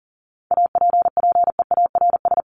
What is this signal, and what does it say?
Morse code, heard as a keyed tone, about 40 wpm, 700 Hz. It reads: APPEARS
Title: ES:S Suffix - Top 100 Words with Base Words 40wpm